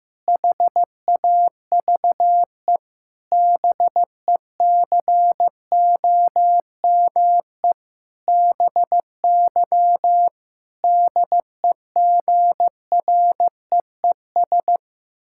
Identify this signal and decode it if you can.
Morse code, heard as a keyed tone, about 15 wpm, 700 Hz. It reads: HAVE BECOME BY DEGREES
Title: Pangram Challenge v2 - 15wpm